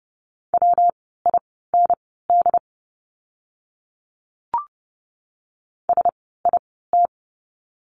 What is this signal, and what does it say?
Morse code, heard as a keyed tone, about 30 wpm, spaced as for 17 wpm, 700 Hz. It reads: WIND SIT
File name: course / 30:10wpm / 115 Lesson 19f - Letter w - Words 30wpm